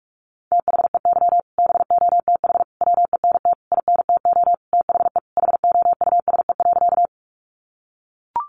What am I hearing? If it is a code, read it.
Morse code, heard as a keyed tone, about 44 wpm, 700 Hz. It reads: THEY BOTH WENT INTO THE HOUSE.
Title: Sentences from Top 600 Words 45wpm